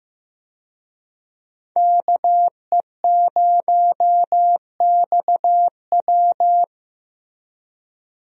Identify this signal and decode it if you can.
Morse code, heard as a keyed tone, about 15 wpm, 700 Hz. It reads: KE0XW
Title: USA Call Signs - 5 Characters - Repeated 2x - 15wpm